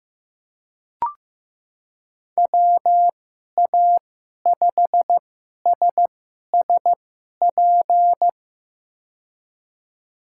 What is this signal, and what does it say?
Morse code, heard as a keyed tone, about 15 wpm, spaced as for 11 wpm, 700 Hz. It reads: WA5SSP
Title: USA Call Signs - 6 Characters - 2x Character Spacing - 15wpm